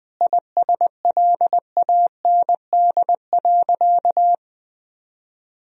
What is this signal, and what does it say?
Morse code, heard as a keyed tone, about 20 wpm, 700 Hz. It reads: ISLAND.
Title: Sentences from Top 1000 Words - Review Part 3 - 20wpm